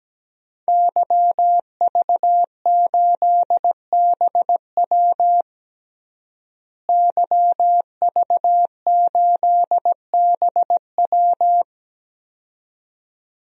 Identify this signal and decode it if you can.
Morse code, heard as a keyed tone, about 17 wpm, 700 Hz. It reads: YV8BW YV8BW
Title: DXCC-100 Call Signs - 5 Characters - Repeated 2x - 17wpm